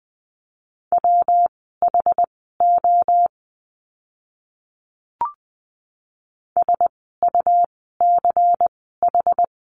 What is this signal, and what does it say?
Morse code, heard as a keyed tone, about 20 wpm, spaced as for 15 wpm, 700 Hz. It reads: WHO SUCH
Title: 146 Lesson 24f - Letter y - Words 20wpm